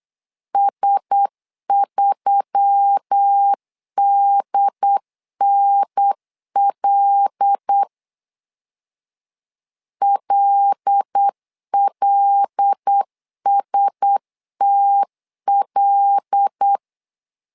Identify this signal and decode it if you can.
Morse code, heard as a keyed tone, about 8 wpm, 800 Hz. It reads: S3DNL LLSTL
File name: LEZIONI / 5 LBVC / 5 L 8wpm 2831